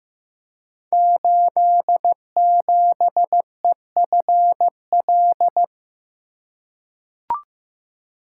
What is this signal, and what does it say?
Morse code, heard as a keyed tone, about 15 wpm, 700 Hz. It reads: 87EFL
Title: Code Groups - 5 Letter:Numbers - 15wpm